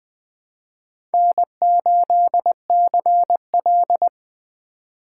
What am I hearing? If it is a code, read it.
Morse code, heard as a keyed tone, about 20 wpm, 700 Hz. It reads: N8CL
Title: Speed Racing - USA Call Signs 17wpm